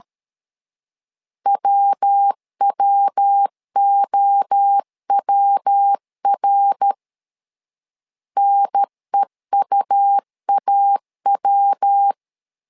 Morse code, heard as a keyed tone, about 13 wpm, 800 Hz. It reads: WWOWR NEUAW
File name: LEZIONI / 3 MWRO / 3 MWRO 12wpm 1730